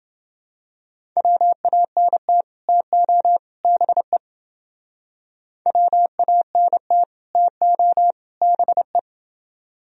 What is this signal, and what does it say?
Morse code, heard as a keyed tone, about 29 wpm, 700 Hz. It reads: WANT TO BE WANT TO BE